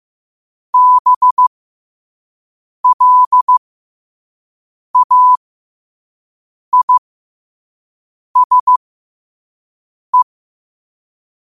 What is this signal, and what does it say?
Morse code, heard as a keyed tone, about 15 wpm, spaced as for 5 wpm, 1000 Hz. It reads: BLAISE
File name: qotdfiles / 5wpm